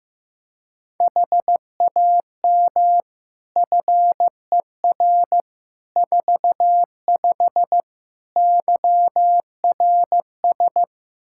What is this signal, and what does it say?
Morse code, heard as a keyed tone, about 15 wpm, 700 Hz. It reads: HAM FER 45 YRS
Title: Practice QSO - Exchange 3 - More Info - 15wpm